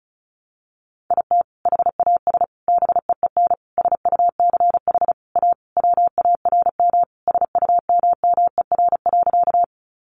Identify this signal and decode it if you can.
Morse code, heard as a keyed tone, about 34 wpm, 700 Hz. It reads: IT HAS BEEN SUCH A WARM SUMMER.
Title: Sentences from Top 1100 Words 35wpm